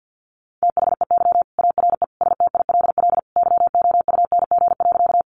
Morse code, heard as a keyed tone, about 50 wpm, 700 Hz. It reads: THEY ARE STILL YOUNG.
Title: Sentences from Top 400 Words 50wpm